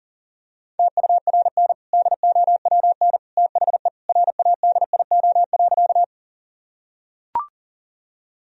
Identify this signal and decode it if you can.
Morse code, heard as a keyed tone, about 40 wpm, 700 Hz. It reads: TURN DOWN THE RADIO.
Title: Sentences from Top 1000 Words - Review Part 3 - 40wpm